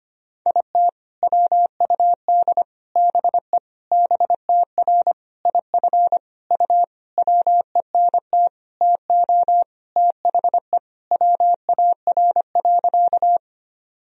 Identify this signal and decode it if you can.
Morse code, heard as a keyed tone, about 25 wpm, 700 Hz. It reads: IT WUD BE BTR IF U WENT TO THE WAR.
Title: Sentences from Top 400 Words and Abbreviations 25wpm